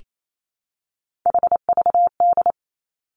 Morse code, heard as a keyed tone, about 28 wpm, 700 Hz. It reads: HVD
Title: Three Challenging Letter-Number Combinations 28wpm